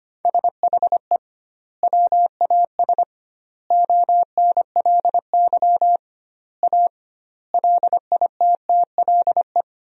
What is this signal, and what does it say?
Morse code, heard as a keyed tone, about 25 wpm, 700 Hz. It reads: SHE WAS ONLY A LITTLE
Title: 2x Word-Spacing - Sentences from Top 400 Words - 25wpm